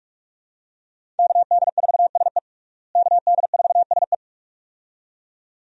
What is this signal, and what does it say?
Morse code, heard as a keyed tone, about 46 wpm, 700 Hz. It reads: KD4SE KD4SE
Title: USA Call Signs - 5 Characters - Repeated 2x - 45wpm